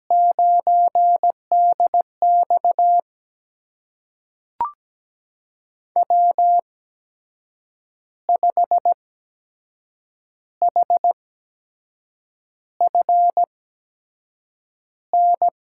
Morse code, heard as a keyed tone, about 17 wpm, spaced as for 6 wpm, 700 Hz. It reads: 9DX W5HFN
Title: USA Call Signs - 5 Characters - 8x Character Spacing - 17wpm